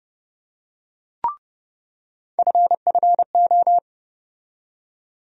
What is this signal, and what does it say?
Morse code, heard as a keyed tone, about 30 wpm, 700 Hz. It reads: FFO
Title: USA Call Sign Postfixes - 3 Letters 30wpm